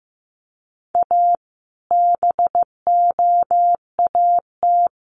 Morse code, heard as a keyed tone, about 15 wpm, 700 Hz. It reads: A BOAT
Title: Sentences from Top 1400 Words 15wpm